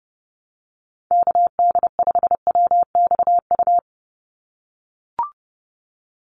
Morse code, heard as a keyed tone, about 29 wpm, 700 Hz. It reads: KD5WXU
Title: USA Call Signs 30wpm